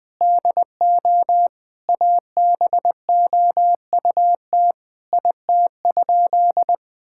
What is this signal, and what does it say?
Morse code, heard as a keyed tone, about 20 wpm, 700 Hz. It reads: DO ABOUT IT?
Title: Sentences from Top 1000 Words - Review Part 6 - 20wpm